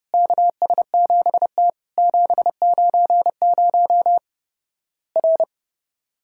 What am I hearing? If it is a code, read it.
Morse code, heard as a keyed tone, about 30 wpm, 700 Hz. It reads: KS7T 790 R
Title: Worked All Europe DX Contest - QTC Exchange - 30wpm